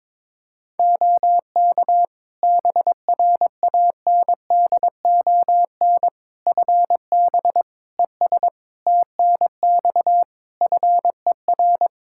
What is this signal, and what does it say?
Morse code, heard as a keyed tone, about 22 wpm, 700 Hz. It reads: OK BRANDON FB ES TNX FER
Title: Practice QSO - Exchange 2 - Info - 22wpm